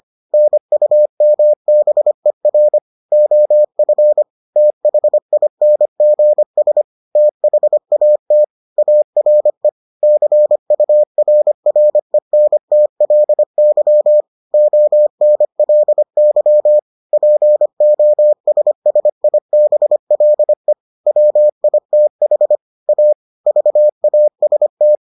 Morse code, heard as a keyed tone, about 25 wpm, 600 Hz. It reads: NUMBER OF THINGS THAT ARE CURRENTLY ONLY POSSIBLE WITH A VAST